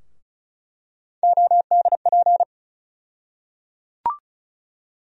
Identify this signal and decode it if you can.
Morse code, heard as a keyed tone, about 36 wpm, 700 Hz. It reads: ODP
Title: USA Call Sign Postfixes - 3 Letters 35wpm